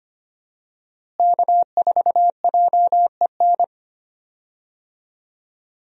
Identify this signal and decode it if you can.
Morse code, heard as a keyed tone, about 25 wpm, 700 Hz. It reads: K4JEN
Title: USA Call Signs - 5 Characters - Repeated 3x - 25wpm